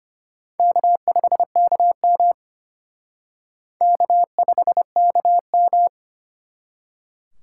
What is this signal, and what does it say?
Morse code, heard as a keyed tone, about 34 wpm, 700 Hz. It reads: K5KM K5KM